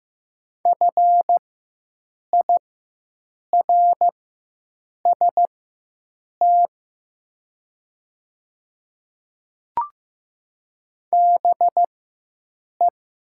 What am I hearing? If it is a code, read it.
Morse code, heard as a keyed tone, about 15 wpm, spaced as for 7 wpm, 700 Hz. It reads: FIRST BE
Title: 4x Character Spacing - Top 100 Words - 15wpm